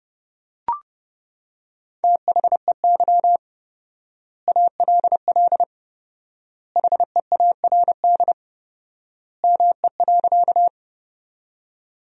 Morse code, heard as a keyed tone, about 32 wpm, 700 Hz. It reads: THEY ALL HEARD ME.
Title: 4x Word-Spacing - Sentences from Top 400 Words - 30wpm